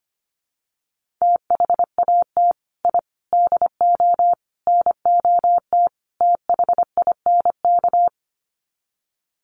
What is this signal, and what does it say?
Morse code, heard as a keyed tone, about 25 wpm, 700 Hz. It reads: THAT I DO NOT THINK
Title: Sets of 5 Words Encore 25wpm